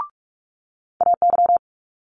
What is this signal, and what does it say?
Morse code, heard as a keyed tone, about 44 wpm, 700 Hz. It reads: AY
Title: Multi-letter Phonemes Quick Fire 45wpm